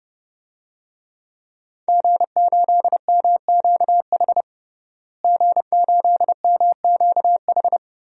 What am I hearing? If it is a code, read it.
Morse code, heard as a keyed tone, about 30 wpm, 700 Hz. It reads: G8MQH G8MQH
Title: DXCC-100 Call Signs - 5 Characters - Repeated 3x - 30wpm